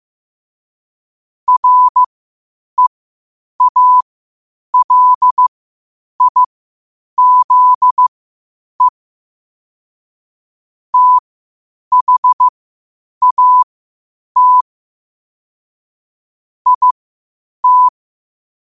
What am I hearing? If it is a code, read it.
Morse code, heard as a keyed tone, about 15 wpm, spaced as for 8 wpm, 1000 Hz. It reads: REALIZE THAT IT
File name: qotdfiles / 8wpm